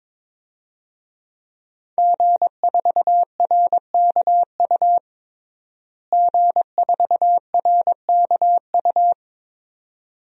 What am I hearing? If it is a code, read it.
Morse code, heard as a keyed tone, about 22 wpm, 700 Hz. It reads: G4RKU G4RKU
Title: DXCC-100 Call Signs - 5 Characters - Repeated 2x - 22wpm